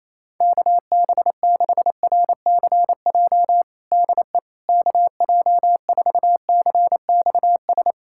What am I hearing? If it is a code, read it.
Morse code, heard as a keyed tone, about 28 wpm, 700 Hz. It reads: KB6RCJ DE KJ4CXS
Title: USA Callsign DE Callsign - 6 Characters - 28wpm